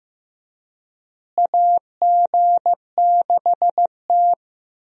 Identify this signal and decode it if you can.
Morse code, heard as a keyed tone, about 15 wpm, 700 Hz. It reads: AG6T
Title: USA Call Signs - 4 Characters - Repeated 2x - 15wpm